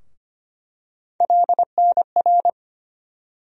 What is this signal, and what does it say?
Morse code, heard as a keyed tone, about 25 wpm, 700 Hz. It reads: LNR
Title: USA Call Sign Postfixes - 3 Letters 25wpm